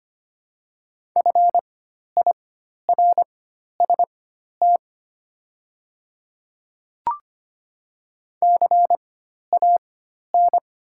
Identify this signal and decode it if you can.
Morse code, heard as a keyed tone, about 25 wpm, spaced as for 12 wpm, 700 Hz. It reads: FIRST CAN
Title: 4x Character Spacing - Top 100 Words - 25wpm